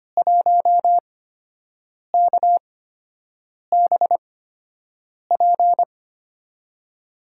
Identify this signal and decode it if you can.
Morse code, heard as a keyed tone, about 25 wpm, spaced as for 7 wpm, 700 Hz. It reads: 1KBP
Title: USA Call Signs - 5 Characters - 8x Character Spacing - 25wpm